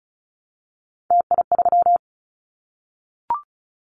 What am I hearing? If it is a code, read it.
Morse code, heard as a keyed tone, about 34 wpm, 700 Hz. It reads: TI3
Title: Callsign Prefixes for Top 100 DXCC - 35wpm